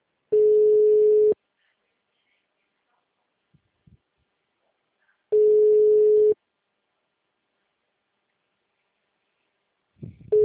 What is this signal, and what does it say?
Ringback tone (Europe, 425 Hz).
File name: out- -2032- - - .23636